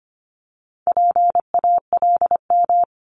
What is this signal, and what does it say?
Morse code, heard as a keyed tone, about 25 wpm, 700 Hz. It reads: PALM
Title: Multi-letter Phonemes Repeated 3x 25wpm